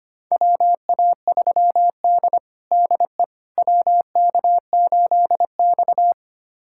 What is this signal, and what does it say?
Morse code, heard as a keyed tone, about 25 wpm, 700 Hz. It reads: WA3D DE WK8X
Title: USA Callsign DE Callsign - 4 Characters - 25wpm